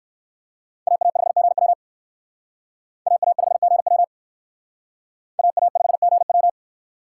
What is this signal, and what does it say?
Morse code, heard as a keyed tone, about 55 wpm, 700 Hz. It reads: AA5GW AA5GW AA5GW
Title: Speed Racing - USA Call Signs 45wpm